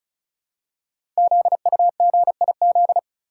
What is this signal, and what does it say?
Morse code, heard as a keyed tone, about 36 wpm, 700 Hz. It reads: ZUGIZ
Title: Code Groups - 5 Letter:Numbers - 35wpm